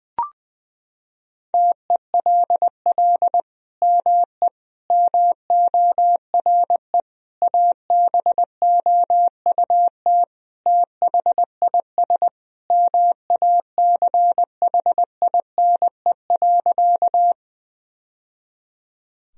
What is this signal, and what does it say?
Morse code, heard as a keyed tone, about 21 wpm, 700 Hz. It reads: TELL ME MORE ABOUT THIS MACHINE.
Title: Sentences from Top 1000 Words - Review Part 2 - 20wpm